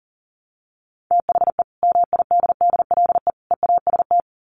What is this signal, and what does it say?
Morse code, heard as a keyed tone, about 40 wpm, 700 Hz. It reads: THE MIDDLE EAST
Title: Sets of 3 Words 40wpm